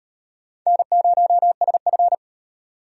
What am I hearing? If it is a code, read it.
Morse code, heard as a keyed tone, about 32 wpm, 700 Hz. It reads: N0SF